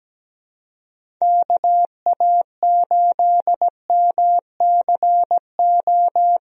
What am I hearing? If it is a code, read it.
Morse code, heard as a keyed tone, about 17 wpm, 700 Hz. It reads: KA8MCO
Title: USA Call Signs - 6 Characters - Repeated 3x - 17wpm